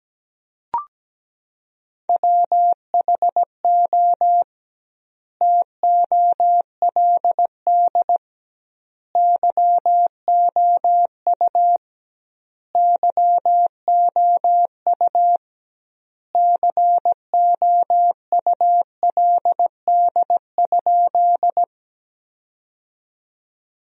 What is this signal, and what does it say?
Morse code, heard as a keyed tone, about 17 wpm, 700 Hz. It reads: WHO TOLD YOU YOU COULD?
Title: 2x Word-Spacing - Sentences from Top 400 Words - 17wpm